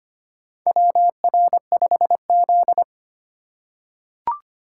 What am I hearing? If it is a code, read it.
Morse code, heard as a keyed tone, about 25 wpm, 700 Hz. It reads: WR5Z
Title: USA Call Signs - 2x Character Spacing - 25wpm